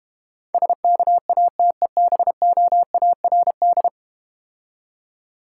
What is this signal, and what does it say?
Morse code, heard as a keyed tone, about 32 wpm, 700 Hz. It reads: SKATEBOARD